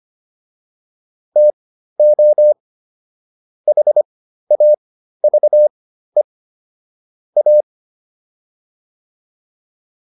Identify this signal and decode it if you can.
Morse code, heard as a keyed tone, about 25 wpm, spaced as for 13 wpm, 600 Hz. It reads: TO HAVE A